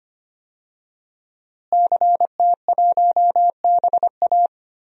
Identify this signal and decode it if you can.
Morse code, heard as a keyed tone, about 25 wpm, 700 Hz. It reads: CT1BA
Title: DXCC-100 Call Signs - 5 Characters - Repeated 3x - 25wpm